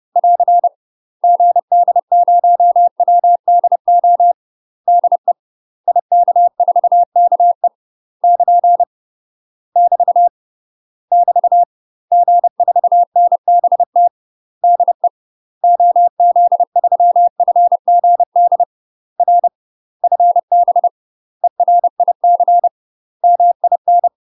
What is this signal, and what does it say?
Morse code, heard as a keyed tone, about 30 wpm, 700 Hz. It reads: <AR> GD0WDO DE IK4KE <KN> <BT> <BT> G4NBT DE OZ3FGD R FB ERIC MIN
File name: UK MASTER QSOs TRAINING / 30WPM / UK MASTER QSOs 0014